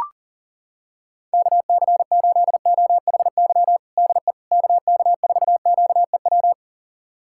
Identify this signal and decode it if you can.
Morse code, heard as a keyed tone, about 40 wpm, 700 Hz. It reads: KC8OHY DE KK4QEW
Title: USA Callsign DE Callsign 40wpm